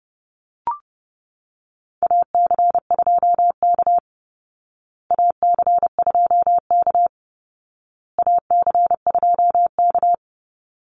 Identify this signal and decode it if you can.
Morse code, heard as a keyed tone, about 30 wpm, 700 Hz. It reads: AC2K AC2K AC2K
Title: Call Signs Repeated 2x - 30wpm